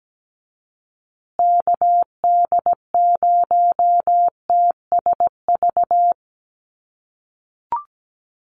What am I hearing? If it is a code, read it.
Morse code, heard as a keyed tone, about 17 wpm, 700 Hz. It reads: KD0TSV
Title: USA Call Signs Repeated 2x 17wpm